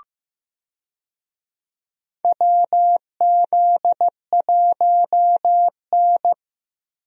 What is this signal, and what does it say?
Morse code, heard as a keyed tone, about 15 wpm, 700 Hz. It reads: WZ1N
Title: USA Call Signs - 4 Characters - 15wpm